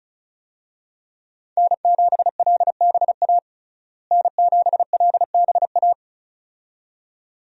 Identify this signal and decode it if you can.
Morse code, heard as a keyed tone, about 36 wpm, 700 Hz. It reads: N7LBA N7LBA
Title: USA Call Signs - 5 Characters - Repeated 2x - 35wpm